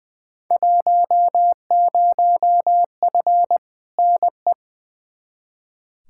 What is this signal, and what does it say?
Morse code, heard as a keyed tone, about 20 wpm, 700 Hz. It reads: 10F NE
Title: Field Day Exchanges Encore - 20wpm